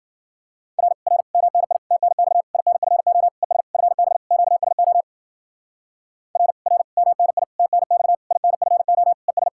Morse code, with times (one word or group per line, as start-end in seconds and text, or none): 0.8-0.9 R
1.1-1.2 R
1.3-1.8 MNI
1.9-2.4 TNX
2.5-3.3 INFO
3.4-3.6 ES
3.7-4.2 FB
4.3-5.0 QSO
6.4-6.5 R
6.7-6.8 R
7.0-7.4 MNI
7.6-8.2 TNX
8.3-9.1 INFO
9.3-9.5 ES